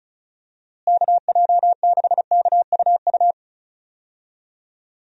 Morse code, heard as a keyed tone, about 34 wpm, 700 Hz. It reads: KJ6KUU
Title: USA Call Signs - 6 Characters - Repeated 3x - 35wpm